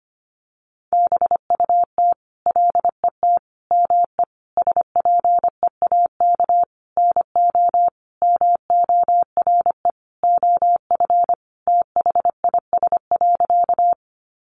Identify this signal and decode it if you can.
Morse code, heard as a keyed tone, about 25 wpm, 700 Hz. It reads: BUT LET ME SPEAK NO MORE OF THIS.
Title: Sentences from Top 1100 Words 25wpm